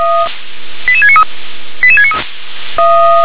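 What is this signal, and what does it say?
Five-tone selective call (ZVEI), heard as quick runs of five tones, tones of about 70 ms. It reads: 88693 88693
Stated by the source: Five-tone ZVEI selective call 88788 received by SDR.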